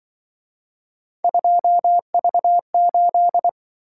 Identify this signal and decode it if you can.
Morse code, heard as a keyed tone, about 25 wpm, 700 Hz. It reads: 2V8
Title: Speed-Racing - Three Letter-Number Combinations - 20wpm